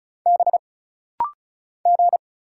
Morse code, heard as a keyed tone, about 36 wpm, 700 Hz. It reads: B G
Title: Single Letters - Rapid-Fire 35wpm